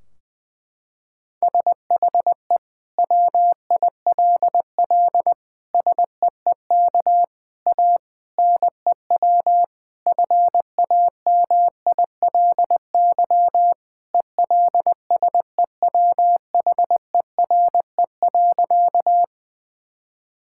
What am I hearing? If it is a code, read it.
Morse code, heard as a keyed tone, about 20 wpm, 700 Hz. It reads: SHE WILL SEEK A NEW FAMILY ELSEWHERE.
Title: Sentences from Top 1600 Words 20wpm